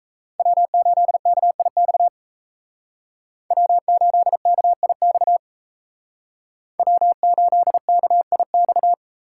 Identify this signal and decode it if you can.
Morse code, heard as a keyed tone, about 33 wpm, 700 Hz. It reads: W8KIX W8KIX W8KIX
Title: Speed Racing - USA 1x3 Call Signs 28wpm